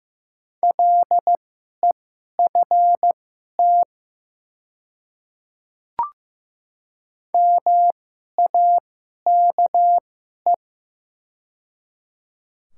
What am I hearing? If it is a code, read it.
Morse code, heard as a keyed tone, about 15 wpm, spaced as for 11 wpm, 700 Hz. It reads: LEFT MAKE
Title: 2x Character Spacing - Top 300 Words - 15wpm